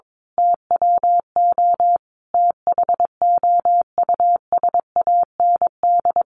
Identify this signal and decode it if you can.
Morse code, heard as a keyed tone, about 22 wpm, 700 Hz. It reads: TWO THOUSAND
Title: A Sound Of Thunder 22wpm